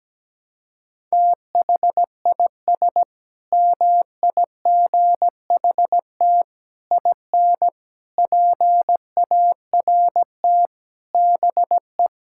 Morse code, heard as a keyed tone, about 17 wpm, 700 Hz. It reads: THIS MIGHT IN PART BE
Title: Sentences from Top 1000 Words - Review Part 5 - 17wpm